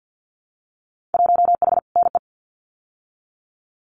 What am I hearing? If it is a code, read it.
Morse code, heard as a keyed tone, about 50 wpm, 700 Hz. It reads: 1H NE
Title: Winter Field Day Exchanges Encore 50wpm